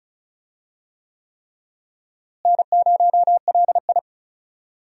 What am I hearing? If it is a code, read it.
Morse code, heard as a keyed tone, about 36 wpm, 700 Hz. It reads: N0LI
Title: USA Call Signs - 4 Characters - Repeated 3x - 35wpm